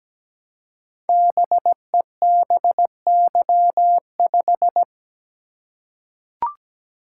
Morse code, heard as a keyed tone, about 17 wpm, 700 Hz. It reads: BEBY5
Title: Code Groups - 5 Letter:Numbers - 17wpm